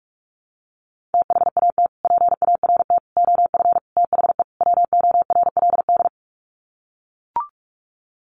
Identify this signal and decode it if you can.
Morse code, heard as a keyed tone, about 44 wpm, 700 Hz. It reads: THAT PART OF THE WORLD
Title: Sets of 5 Words Encore2 45wpm